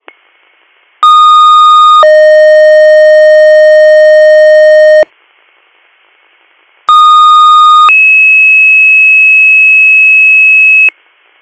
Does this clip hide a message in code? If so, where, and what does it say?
Yes: two-tone page from 1.0 to 10.9 s, heard as a one-second tone, then a three-second tone, about 1230 then 620 Hz, about 1230 then 2600 Hz, with noise.